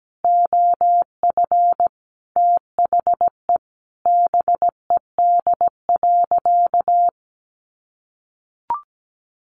Morse code, heard as a keyed tone, about 17 wpm, 700 Hz. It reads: OF THE BED.
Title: Sentences from Top 900 Words 17wpm